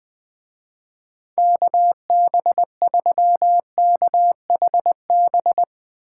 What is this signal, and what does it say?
Morse code, heard as a keyed tone, about 20 wpm, 700 Hz. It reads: KB3KHB